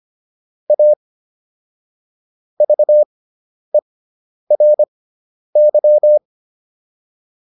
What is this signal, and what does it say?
Morse code, heard as a keyed tone, about 25 wpm, spaced as for 10 wpm, 600 Hz. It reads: A VERY